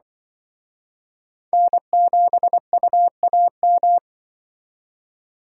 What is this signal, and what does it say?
Morse code, heard as a keyed tone, about 23 wpm, 700 Hz. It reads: N7UAM
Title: Speed Racing - USA 1x3 Call Signs 20wpm